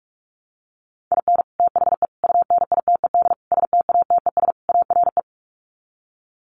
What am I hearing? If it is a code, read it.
Morse code, heard as a keyed tone, about 44 wpm, 700 Hz. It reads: IN THE UNITED STATES ARE
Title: Sets of 5 Words Encore 45wpm